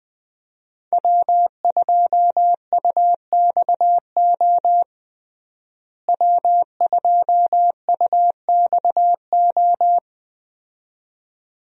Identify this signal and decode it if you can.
Morse code, heard as a keyed tone, about 20 wpm, 700 Hz. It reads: W2UXO W2UXO
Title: USA Call Signs - 5 Characters - Repeated 2x - 20wpm